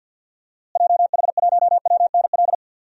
Morse code, heard as a keyed tone, about 47 wpm, 700 Hz. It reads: JH1WNL
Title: DXCC-100 Call Signs - 6 Characters - Repeated 3x - 50wpm